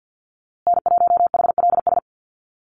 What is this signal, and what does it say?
Morse code, heard as a keyed tone, about 48 wpm, 700 Hz. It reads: N1HLS
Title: USA Call Signs Repeated 3x 50wpm